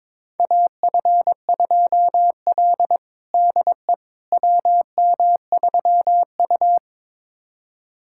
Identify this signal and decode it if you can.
Morse code, heard as a keyed tone, about 21 wpm, 700 Hz. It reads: AF2L DE WM3U
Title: USA Callsign DE Callsign - 4 Characters - 22wpm